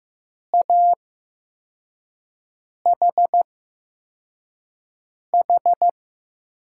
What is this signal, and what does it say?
Morse code, heard as a keyed tone, about 15 wpm, spaced as for 4 wpm, 700 Hz. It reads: AHH
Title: USA Call Signs - 6 Characters - 8x Character Spacing - 15wpm